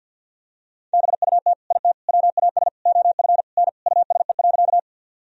Morse code, heard as a keyed tone, about 52 wpm, 700 Hz. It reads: BUT IT WAS OF N USE.
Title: Sentences from Top 100 Words and Abbreviations 50wpm